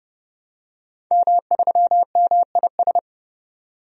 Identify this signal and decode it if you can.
Morse code, heard as a keyed tone, about 30 wpm, 700 Hz. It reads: M3MIS
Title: DXCC-100 Call Signs - 5 Characters - 30wpm